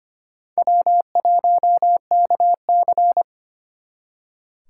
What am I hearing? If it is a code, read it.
Morse code, heard as a keyed tone, about 32 wpm, 700 Hz. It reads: W1KC